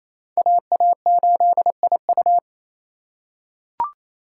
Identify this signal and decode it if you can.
Morse code, heard as a keyed tone, about 28 wpm, 700 Hz. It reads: AA8IU
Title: USA Call Signs - 5 Characters - 2x Character Spacing - 28wpm